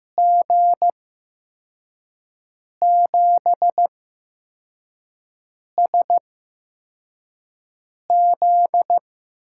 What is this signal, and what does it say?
Morse code, heard as a keyed tone, about 15 wpm, spaced as for 4 wpm, 700 Hz. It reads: G7SZ
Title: USA Call Signs - 5 Characters - 8x Character Spacing - 15wpm